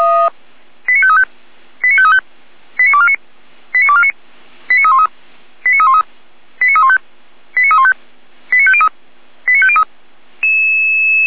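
Five-tone selective call (ZVEI), heard as quick runs of five tones, tones of about 70 ms. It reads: 89535 89535 89249 89249 89313 89313 89315 89315 89503 89503
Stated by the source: ZVEI five-tone call received by SDR.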